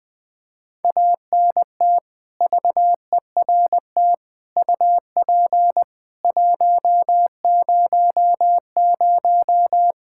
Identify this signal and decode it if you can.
Morse code, heard as a keyed tone, about 20 wpm, 700 Hz. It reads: ANT VERT UP 100
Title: Practice QSO - Exchange 2 - Info - 20wpm